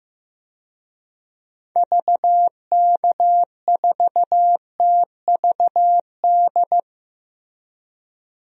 Morse code, heard as a keyed tone, about 16 wpm, 700 Hz. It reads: VK4TVD
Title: DXCC-100 Call Signs - 6 Characters - Repeated 2x - 15wpm